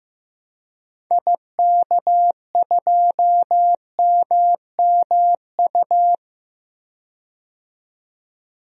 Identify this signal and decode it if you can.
Morse code, heard as a keyed tone, about 15 wpm, 700 Hz. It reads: IK2MMU
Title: DXCC-100 Call Signs - 6 Characters - 15wpm